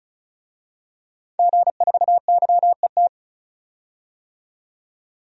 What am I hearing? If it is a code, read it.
Morse code, heard as a keyed tone, about 36 wpm, 700 Hz. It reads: G4YET
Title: DXCC-100 Call Signs - 5 Characters - 35wpm